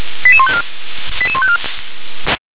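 Five-tone selective call (ZVEI), heard as quick runs of five tones, tones of about 70 ms. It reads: 88154 88154
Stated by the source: Received on an SDR; ZVEI five-tone call 88742.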